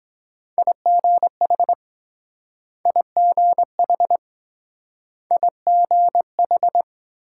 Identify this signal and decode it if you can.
Morse code, heard as a keyed tone, about 23 wpm, 700 Hz. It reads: IGH IGH IGH